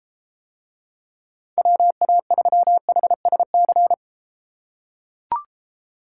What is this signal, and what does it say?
Morse code, heard as a keyed tone, about 33 wpm, 700 Hz. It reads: WA3HSC